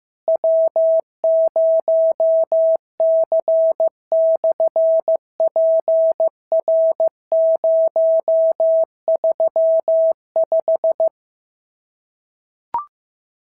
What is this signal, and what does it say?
Morse code, heard as a keyed tone, about 15 wpm, 650 Hz. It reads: W0C/PR035
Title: SOTA QSO Practice 15wpm